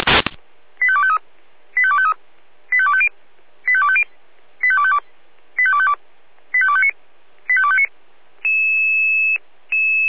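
Five-tone selective call (ZVEI), heard as quick runs of five tones, tones of about 70 ms. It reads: 85253 85253 85250 85250 85252 85252 85259 85259
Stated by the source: Five-tone ZVEI selective call received on SDR.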